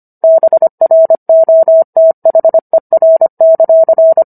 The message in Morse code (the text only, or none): BROTHER;